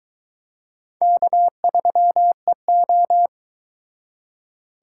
Morse code, heard as a keyed tone, about 24 wpm, 700 Hz. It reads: K3EO